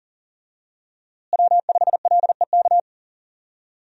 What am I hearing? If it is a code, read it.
Morse code, heard as a keyed tone, about 40 wpm, 700 Hz. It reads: W5LEK
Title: USA Call Signs - 5 Characters - Repeated 3x - 40wpm